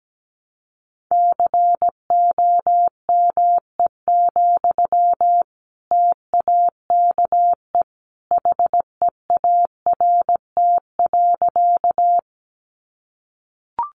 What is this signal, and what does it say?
Morse code, heard as a keyed tone, about 17 wpm, 700 Hz. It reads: COME, TAKE HEART.
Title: Sentences from Top 1200 Words 17wpm